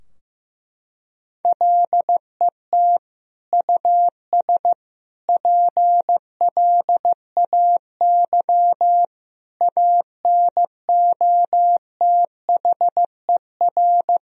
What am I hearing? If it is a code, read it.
Morse code, heard as a keyed tone, about 15 wpm, 700 Hz. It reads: LET US PLAY ANOTHER
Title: Sentences from Top 1000 Words - Review Part 3 - 15wpm